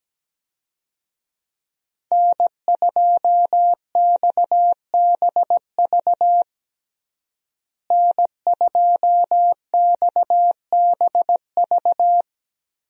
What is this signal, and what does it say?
Morse code, heard as a keyed tone, about 17 wpm, 700 Hz. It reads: N2XBV N2XBV
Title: USA Call Signs - 5 Characters - Repeated 3x - 17wpm